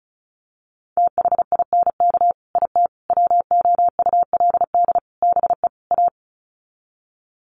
Morse code, heard as a keyed tone, about 34 wpm, 700 Hz. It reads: THINK IT WOULD BE A
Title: Sets of 5 Words Encore2 35wpm